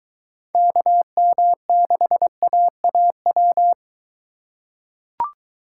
Morse code, heard as a keyed tone, about 23 wpm, 700 Hz. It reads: KM6AAW